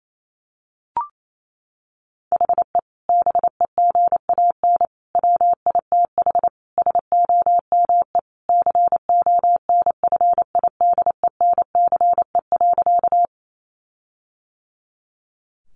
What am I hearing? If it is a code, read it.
Morse code, heard as a keyed tone, about 28 wpm, 700 Hz. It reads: HE BEGAN WITH SOME CONFIDENCE.
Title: Sentences from Top 1500 Words 28wpm